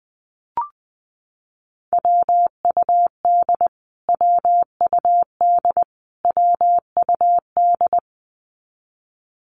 Morse code, heard as a keyed tone, about 20 wpm, 700 Hz. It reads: WUD WUD WUD
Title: Abbreviations Repeated 3x - 20wpm